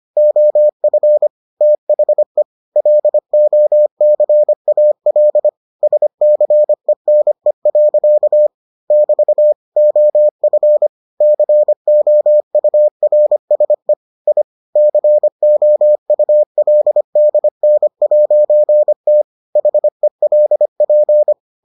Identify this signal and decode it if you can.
Morse code, heard as a keyed tone, about 25 wpm, 600 Hz. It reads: OF THE LOCAL SCENE. <BT> OF COURSE I COULDN'T HELP